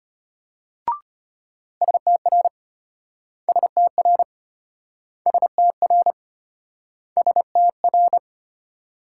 Morse code, heard as a keyed tone, about 34 wpm, 700 Hz. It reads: STR STR STR STR